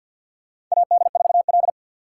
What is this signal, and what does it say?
Morse code, heard as a keyed tone, about 47 wpm, 700 Hz. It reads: AD4L
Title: USA Call Signs - 3x Character Spacing - 50wpm